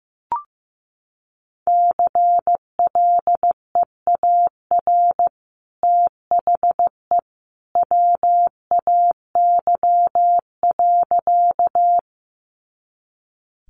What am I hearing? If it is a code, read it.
Morse code, heard as a keyed tone, about 15 wpm, 700 Hz. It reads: CLEAR THE WAY.
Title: Sentences from Top 500 Words 15wpm